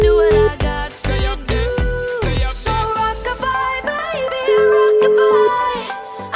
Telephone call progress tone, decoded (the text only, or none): ringback tone